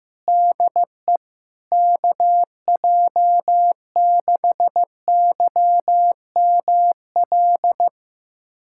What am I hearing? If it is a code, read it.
Morse code, heard as a keyed tone, about 15 wpm, 700 Hz. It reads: DE KJ6YML
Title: USA Callsign DE Callsign - 6 Characters - 15wpm